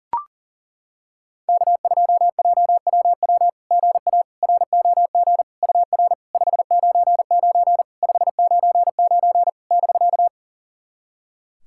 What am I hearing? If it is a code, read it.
Morse code, heard as a keyed tone, about 40 wpm, 690 Hz. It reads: K2JWW GA ROG UR 599 599 <BK>
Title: SOTA QSO Practice 40wpm